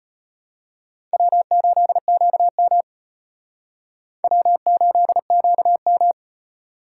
Morse code, heard as a keyed tone, about 32 wpm, 700 Hz. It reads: W8QM W8QM